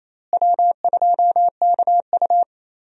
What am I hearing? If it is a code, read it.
Morse code, heard as a keyed tone, about 28 wpm, 700 Hz. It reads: W2KU
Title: USA Call Signs - 4 Characters - Repeated 3x - 28wpm